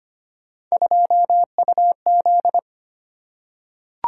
Morse code, heard as a keyed tone, about 25 wpm, 700 Hz. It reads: 2UZ
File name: Three Letter-Number Combinations - 25wpm